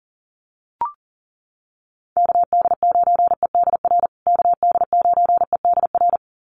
Morse code, heard as a keyed tone, about 37 wpm, 700 Hz. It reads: KD9EDR KD9EDR
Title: USA Call Signs Repeated 2x 40wpm